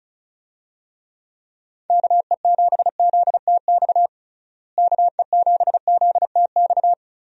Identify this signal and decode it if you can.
Morse code, heard as a keyed tone, about 36 wpm, 700 Hz. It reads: KE7ZTX KE7ZTX
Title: USA Call Signs - 6 Characters - Repeated 3x - 35wpm